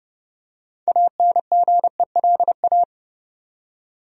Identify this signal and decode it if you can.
Morse code, heard as a keyed tone, about 30 wpm, 700 Hz. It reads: ANGELA
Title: USA Common Names - 30wpm